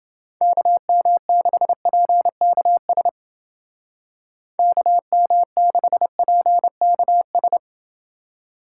Morse code, heard as a keyed tone, about 28 wpm, 700 Hz. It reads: KM6PKS KM6PKS